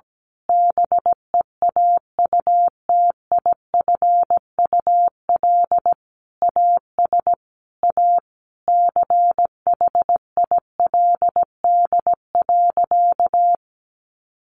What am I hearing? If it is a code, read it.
Morse code, heard as a keyed tone, about 17 wpm, 700 Hz. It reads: BEAUTIFUL AS A CHILD.
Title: Sentences from Top 1500 Words 17wpm